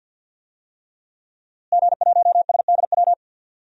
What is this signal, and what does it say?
Morse code, heard as a keyed tone, about 53 wpm, 700 Hz. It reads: G1SDW
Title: DXCC-100 Call Signs - 5 Characters - Repeated 3x - 50wpm